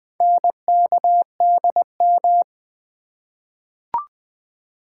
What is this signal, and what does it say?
Morse code, heard as a keyed tone, about 20 wpm, 700 Hz. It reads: NKDM
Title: Four Letter-Number Combinations - 20wpm